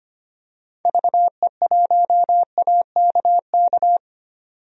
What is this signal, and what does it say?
Morse code, heard as a keyed tone, about 25 wpm, 700 Hz. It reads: VE1AKK
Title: DXCC-100 Call Signs - 6 Characters - 25wpm